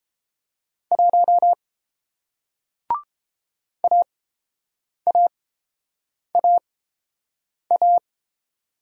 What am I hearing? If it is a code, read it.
Morse code, heard as a keyed tone, about 30 wpm, spaced as for 7 wpm, 690 Hz. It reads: 1 AAAA